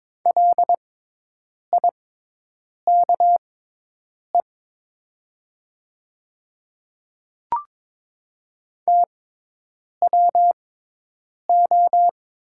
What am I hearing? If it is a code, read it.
Morse code, heard as a keyed tone, about 22 wpm, spaced as for 8 wpm, 700 Hz. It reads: LIKE TWO
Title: 6x Character Spacing - Top 100 Words - 22wpm